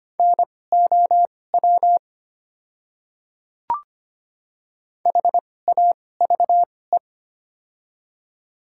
Morse code, heard as a keyed tone, about 25 wpm, spaced as for 18 wpm, 690 Hz. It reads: NOW HAVE
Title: 2x Character Spacing - Top 100 Words - 25wpm